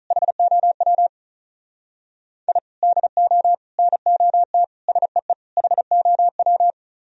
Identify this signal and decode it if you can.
Morse code, heard as a keyed tone, about 45 wpm, 700 Hz. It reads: HOW I DO NOT SEE HOW